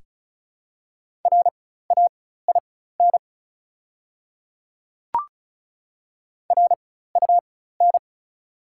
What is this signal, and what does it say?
Morse code, heard as a keyed tone, about 36 wpm, spaced as for 16 wpm, 690 Hz. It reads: RAIN RUN
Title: 4x Character Spacing - Top 300 Words - 35wpm